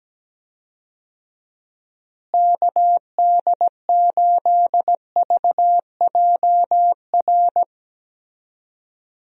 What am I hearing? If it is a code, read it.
Morse code, heard as a keyed tone, about 17 wpm, 700 Hz. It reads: KD8VJR
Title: USA Call Signs - 6 Characters - Repeated 2x - 17wpm